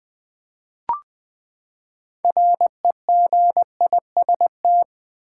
Morse code, heard as a keyed tone, about 20 wpm, 700 Hz. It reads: REGIST
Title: Build A Word - 6 Letters At A Time - 12-Letter Words - 20wpm